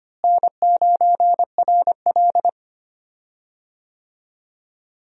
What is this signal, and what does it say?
Morse code, heard as a keyed tone, about 25 wpm, 700 Hz. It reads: N9RL
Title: USA Call Signs - 4 Characters - Repeated 2x - 25wpm